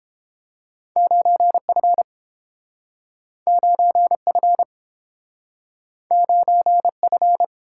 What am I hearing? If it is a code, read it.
Morse code, heard as a keyed tone, about 30 wpm, 700 Hz. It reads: 9F 9F 9F